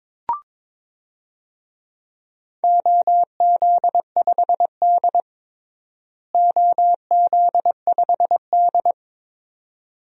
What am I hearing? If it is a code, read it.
Morse code, heard as a keyed tone, about 22 wpm, 700 Hz. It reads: OZ5D OZ5D
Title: DXCC-100 Call Signs - 4 Characters - Repeated 2x - 22wpm